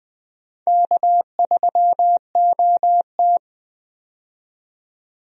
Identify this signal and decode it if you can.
Morse code, heard as a keyed tone, about 20 wpm, 700 Hz. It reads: K3OT
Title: USA Call Signs - 4 Characters - 20wpm